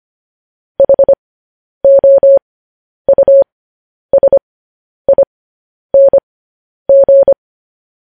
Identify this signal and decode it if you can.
Morse code, heard as a keyed tone, about 25 wpm, spaced as for 10 wpm, 550 Hz. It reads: HOUSING
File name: podcast / News headlines-1x-25x10